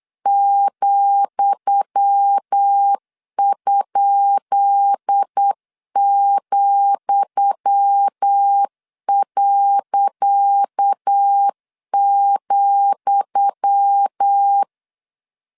Morse code, heard as a keyed tone, about 8 wpm, 800 Hz. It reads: ,?,.,
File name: LEZIONI / 10 SIGNS / 10D COMA 8wpm 3749